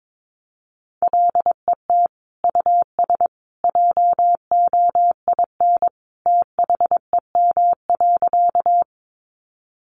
Morse code, heard as a keyed tone, about 22 wpm, 700 Hz. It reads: LET US JOIN THEM.
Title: Sentences from Top 1400 Words 22wpm